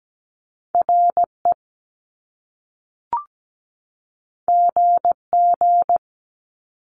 Morse code, heard as a keyed tone, about 17 wpm, 700 Hz. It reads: RE GG
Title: Top Two Letter Combinations in Words 17wpm